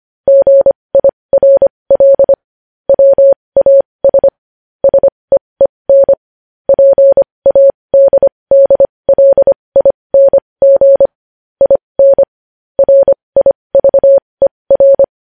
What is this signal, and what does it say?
Morse code, heard as a keyed tone, about 25 wpm, spaced as for 20 wpm, 550 Hz. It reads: GIRL WAS SEEN PADDLING IN RIVER
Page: news-headlines-in-morse-code-at-20-wpm